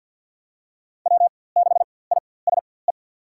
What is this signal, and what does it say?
Morse code, heard as a keyed tone, about 52 wpm, spaced as for 24 wpm, 700 Hz. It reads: W6ISE
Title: USA Call Signs - 5 Characters - 4x Character Spacing - 50wpm